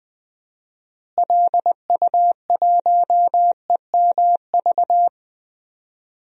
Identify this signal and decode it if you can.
Morse code, heard as a keyed tone, about 20 wpm, 700 Hz. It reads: LU1EMV
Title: DXCC-100 Call Signs - 6 Characters - Repeated 2x - 20wpm